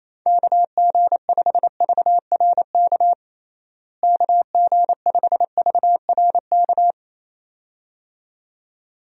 Morse code, heard as a keyed tone, about 27 wpm, 700 Hz. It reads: KG5VRK KG5VRK
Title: USA Call Signs - 6 Characters - Repeated 2x - 28wpm